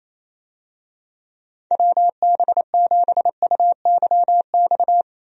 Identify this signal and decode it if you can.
Morse code, heard as a keyed tone, about 28 wpm, 700 Hz. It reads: WB7UYX
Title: USA Call Signs - 6 Characters - Repeated 3x - 28wpm